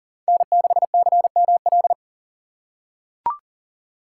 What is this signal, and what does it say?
Morse code, heard as a keyed tone, about 40 wpm, 700 Hz. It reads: N6CML
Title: USA Call Signs - 5 Characters - 6x Character Spacing - 40wpm